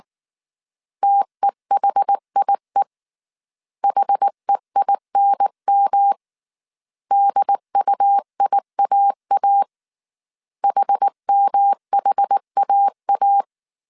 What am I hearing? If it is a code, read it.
Morse code, heard as a keyed tone, about 19 wpm, 800 Hz. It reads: TEHIE HEINM DUIAA HMHAA
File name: CW / DET / 1 3 1500C S B 18wpm 1916